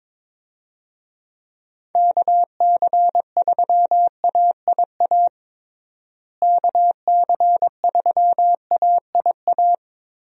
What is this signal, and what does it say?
Morse code, heard as a keyed tone, about 22 wpm, 700 Hz. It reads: KC3AIA KC3AIA
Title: USA Call Signs - 6 Characters - Repeated 3x - 22wpm